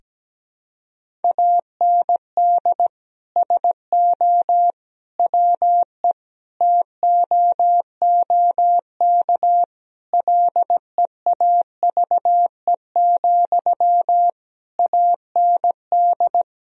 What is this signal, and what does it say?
Morse code, heard as a keyed tone, about 17 wpm, 700 Hz. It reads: AND SO WE TOOK LEAVE, AND
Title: Sentences from Top 1000 Words - Review Part 1 - 17wpm